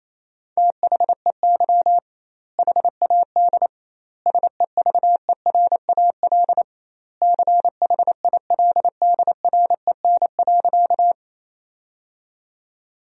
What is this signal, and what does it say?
Morse code, heard as a keyed tone, about 28 wpm, 700 Hz. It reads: THEY HAD SEVERAL CHILDREN.
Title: 2x Word-Spacing - Sentences from Top 400 Words - 28wpm